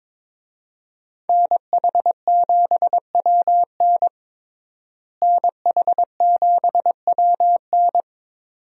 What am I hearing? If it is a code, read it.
Morse code, heard as a keyed tone, about 21 wpm, 700 Hz. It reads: NH7WN NH7WN
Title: USA Call Signs - 5 Characters - Repeated 2x - 22wpm